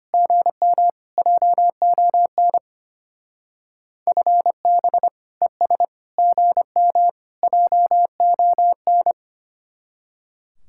Morse code, heard as a keyed tone, about 29 wpm, 700 Hz. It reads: GM JON FB ES GM JON
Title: Speed-Racing Common QSO Phrases - 25wpm